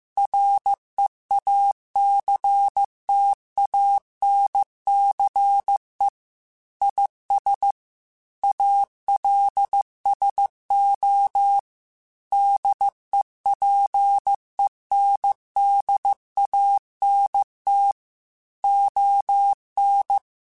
Morse code, heard as a keyed tone, about 15 wpm, 800 Hz. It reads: REACTANCE IS ALSO DEPENDANT ON
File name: Technical Subjects / Technical Subjects-mp3s / XL @13